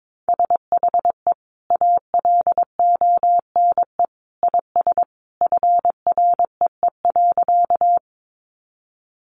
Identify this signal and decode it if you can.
Morse code, heard as a keyed tone, about 22 wpm, 700 Hz. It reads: SHE ALONE IS FREE.
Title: Sentences from Top 1000 Words 22wpm